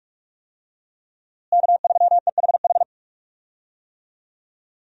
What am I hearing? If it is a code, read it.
Morse code, heard as a keyed tone, about 41 wpm, 700 Hz. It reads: K3EHH
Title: USA Call Signs - 5 Characters - Repeated 3x - 45wpm